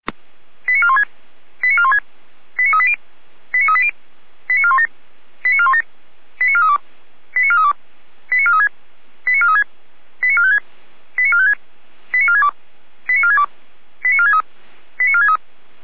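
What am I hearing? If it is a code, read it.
Five-tone selective call (ZVEI), heard as quick runs of five tones, tones of about 70 ms. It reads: 89416 89416 89380 89380 89417 89417 89432 89432 89436 89436 89456 89456 89462 89462 89463 89463
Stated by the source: ZVEI five-tone selective call received on SDR.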